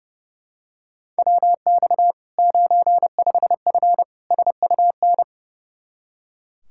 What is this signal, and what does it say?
Morse code, heard as a keyed tone, about 30 wpm, 700 Hz. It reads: WX 95F SUN
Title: Practice QSO - Exchange 2 - Info - 30wpm